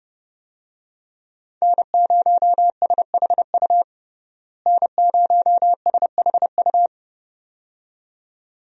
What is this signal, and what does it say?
Morse code, heard as a keyed tone, about 30 wpm, 700 Hz. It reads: N0SHU N0SHU
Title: USA Call Signs - 5 Characters - Repeated 2x - 30wpm